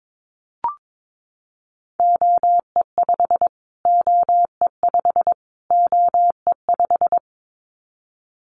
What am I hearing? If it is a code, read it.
Morse code, heard as a keyed tone, about 22 wpm, 700 Hz. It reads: OE5 OE5 OE5
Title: Callsign Prefixes for Top 25 DXCC - Repeated 3x - 22wpm